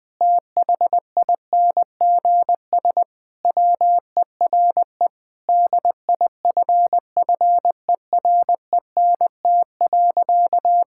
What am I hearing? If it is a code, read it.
Morse code, heard as a keyed tone, about 20 wpm, 700 Hz. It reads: THINGS WERE DIFFERENT.
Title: Sentences with 5-Word Phrases - Spoken Beginning - 20wpm